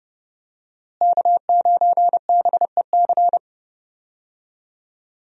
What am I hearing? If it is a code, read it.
Morse code, heard as a keyed tone, about 30 wpm, 700 Hz. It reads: K9BEC
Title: USA Call Signs - 5 Characters - Repeated 3x - 30wpm